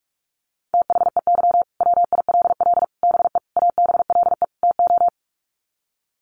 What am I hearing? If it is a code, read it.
Morse code, heard as a keyed tone, about 44 wpm, 700 Hz. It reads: THEY WILL BE ABLE TO
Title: Sets of 5 Words Encore 45wpm